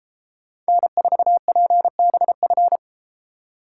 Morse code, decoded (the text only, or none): N4PBF